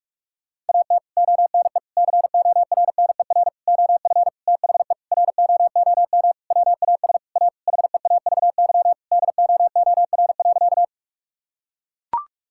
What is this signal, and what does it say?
Morse code, heard as a keyed tone, about 41 wpm, 700 Hz. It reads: AT ONE CORNER OF THE ROOM WAS A HEAVY DOOR.
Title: Sentences from Top 1600 Words 45wpm